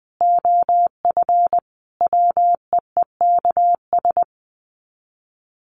Morse code, heard as a keyed tone, about 20 wpm, 700 Hz. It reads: OF WEEKS
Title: Sets of 5 Words Encore2 20wpm